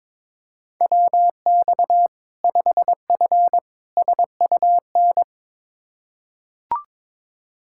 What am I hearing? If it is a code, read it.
Morse code, heard as a keyed tone, about 22 wpm, 700 Hz. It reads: WX 5F SUN
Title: Practice QSO - Exchange 2 - Info - 22wpm